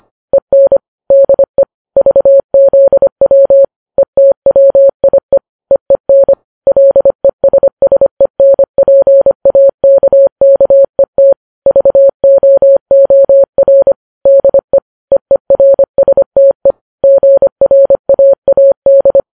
Morse code, heard as a keyed tone, about 25 wpm, 550 Hz. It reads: EN DE VZW ETWIE EEN LESSENPAKKET VOOR DE EERSTE GRAAD
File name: Files Web tls / Morse on9fv / morse25kps